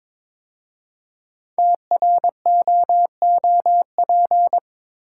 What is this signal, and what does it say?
Morse code, heard as a keyed tone, about 22 wpm, 700 Hz. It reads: TROOP
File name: 5 Letter Words - First 3 Letters Given - 22wpm